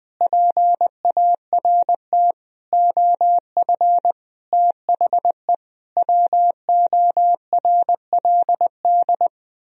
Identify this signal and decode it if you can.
Morse code, heard as a keyed tone, about 20 wpm, 700 Hz. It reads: PART OF THE WORLD
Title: Sentences from Top 1000 Words - Review Part 3 - 20wpm